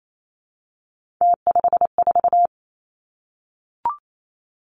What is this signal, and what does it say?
Morse code, heard as a keyed tone, about 28 wpm, 700 Hz. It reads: T54
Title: Three Challenging Letter-Number Combinations 28wpm